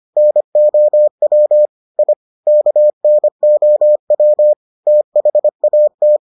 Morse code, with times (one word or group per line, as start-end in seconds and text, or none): 0.2-1.7 NOW
2.0-2.1 I
2.5-4.5 KNOW
4.9-6.2 THAT